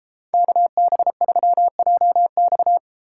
Morse code, heard as a keyed tone, about 33 wpm, 700 Hz. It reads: KB3JX